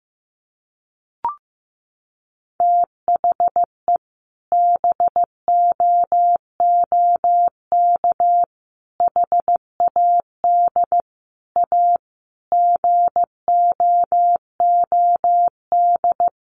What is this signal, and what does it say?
Morse code, heard as a keyed tone, about 15 wpm, 700 Hz. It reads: THE BOOK HAD A GOOD
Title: Sentences from Top 1400 Words 15wpm